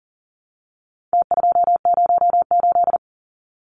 Morse code, heard as a keyed tone, about 40 wpm, 700 Hz. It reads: T208
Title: Numbers - Contest Style - 4-Digit 40wpm